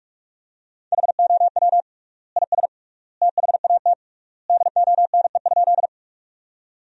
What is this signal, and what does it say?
Morse code, heard as a keyed tone, about 41 wpm, 700 Hz. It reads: HOW IS THAT DONE?
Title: 3x Word-Spacing - Sentences from Top 500 Words - 45wpm